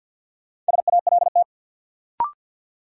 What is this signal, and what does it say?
Morse code, heard as a keyed tone, about 51 wpm, 700 Hz. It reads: SALT
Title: Sentences from Top 800 Words - New Word Drill - 50wpm